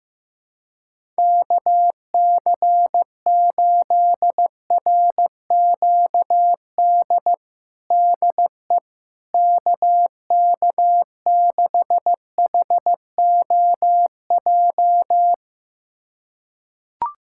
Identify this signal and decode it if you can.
Morse code, heard as a keyed tone, about 15 wpm, 700 Hz. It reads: KC8RQD DE KK6HOJ
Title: USA Callsign DE Callsign - 6 Characters - 15wpm